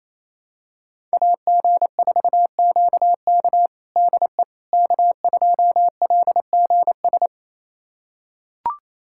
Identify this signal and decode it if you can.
Morse code, heard as a keyed tone, about 28 wpm, 700 Hz. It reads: AG4QK DE K2LGS
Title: USA Callsign DE Callsign - 5 Characters - 28wpm